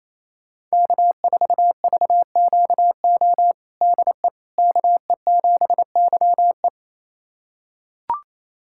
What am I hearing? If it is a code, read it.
Morse code, heard as a keyed tone, about 27 wpm, 700 Hz. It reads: K4VQO DE KE7YE
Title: USA Callsign DE Callsign - 5 Characters - 28wpm